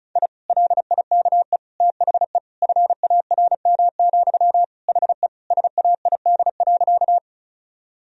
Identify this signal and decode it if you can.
Morse code, heard as a keyed tone, about 36 wpm, 700 Hz. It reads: I LIKE THE FARM, HE SAID.
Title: Sentences from Top 1000 Words - Review Part 2 - 35wpm